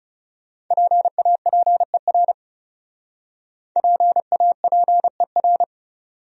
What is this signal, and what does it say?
Morse code, heard as a keyed tone, about 40 wpm, 700 Hz. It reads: PAPER PAPER